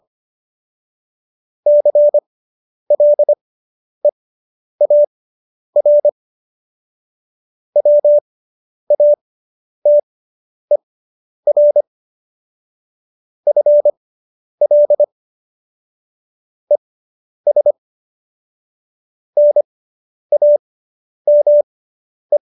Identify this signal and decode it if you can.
Morse code, heard as a keyed tone, about 25 wpm, spaced as for 10 wpm, 590 Hz. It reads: CLEAR WATER FL ES NAME